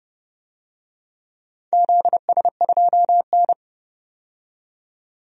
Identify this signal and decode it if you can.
Morse code, heard as a keyed tone, about 30 wpm, 700 Hz. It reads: ZS2N
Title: DXCC-100 Call Signs - 4 Characters - 30wpm